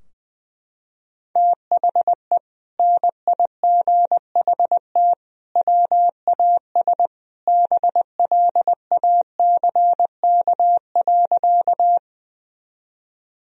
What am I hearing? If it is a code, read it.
Morse code, heard as a keyed tone, about 20 wpm, 700 Hz. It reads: THE NIGHT WAS BLACK.
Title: Sentences from Top 1000 Words - Review Part 2 - 20wpm